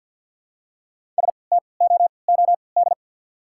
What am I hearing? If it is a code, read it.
Morse code, heard as a keyed tone, about 48 wpm, spaced as for 28 wpm, 700 Hz. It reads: STOOD
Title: 3x Character Spacing - Top 500 Words - 50wpm